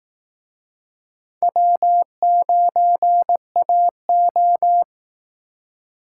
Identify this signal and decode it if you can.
Morse code, heard as a keyed tone, about 19 wpm, 700 Hz. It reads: W9AO
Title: Speed Racing - USA 1x2 Call Signs 15wpm